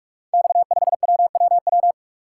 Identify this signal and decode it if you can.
Morse code, heard as a keyed tone, about 44 wpm, 700 Hz. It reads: X5WWW
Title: Code Groups - 5 Letter:Numbers - 45wpm